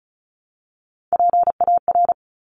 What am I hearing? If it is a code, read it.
Morse code, heard as a keyed tone, about 36 wpm, 690 Hz. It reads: PAR
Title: Top Three Letter Combinations in Words 35wpm